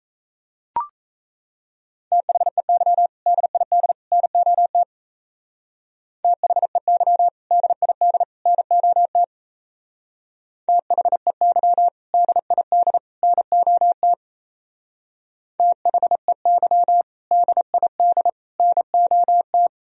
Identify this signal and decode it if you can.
Morse code, heard as a keyed tone, about 35 wpm, 700 Hz. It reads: THEY DID NOT THEY DID NOT THEY DID NOT THEY DID NOT